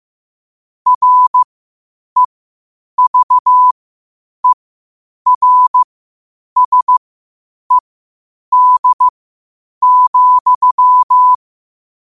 Morse code, heard as a keyed tone, about 15 wpm, spaced as for 8 wpm, 1000 Hz. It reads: REVERSED,
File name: qotdfiles / 8wpm